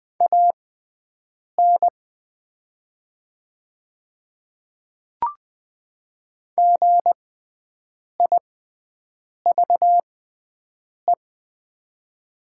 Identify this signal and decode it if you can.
Morse code, heard as a keyed tone, about 20 wpm, spaced as for 7 wpm, 700 Hz. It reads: AN GIVE
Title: 6x Character Spacing - Top 200 Words - 20wpm